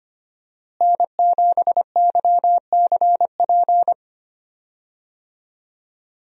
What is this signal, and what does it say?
Morse code, heard as a keyed tone, about 25 wpm, 700 Hz. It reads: N7YCP
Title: USA Call Signs - 5 Characters - Repeated 2x - 25wpm